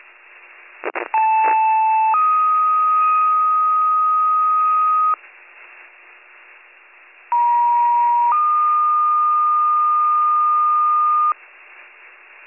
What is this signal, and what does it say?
Two-tone page, heard as a one-second tone, then a three-second tone, about 880 then 1250 Hz, about 950 then 1220 Hz. It is heard from 1.1 to 11.3 s.